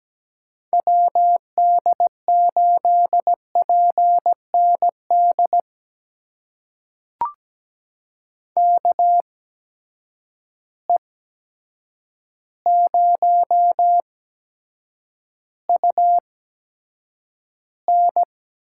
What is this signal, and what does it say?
Morse code, heard as a keyed tone, about 17 wpm, spaced as for 8 wpm, 700 Hz. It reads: WD8PND KE0UN
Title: USA Call Signs - 6 Characters - 8x Character Spacing - 17wpm